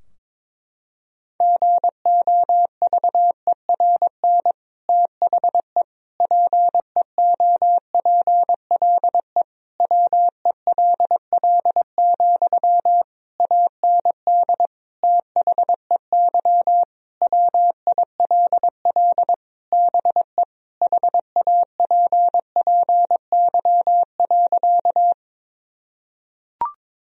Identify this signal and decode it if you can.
Morse code, heard as a keyed tone, about 21 wpm, 700 Hz. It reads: GOVERN THE PEOPLE WELL, AND THEY WILL BE HAPPY.
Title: Sentences from Top 1000 Words - Review Part 2 - 22wpm